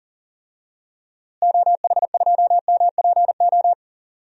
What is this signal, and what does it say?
Morse code, heard as a keyed tone, about 40 wpm, 700 Hz. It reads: OH2MPO
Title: DXCC-100 Call Signs - 6 Characters - Repeated 3x - 40wpm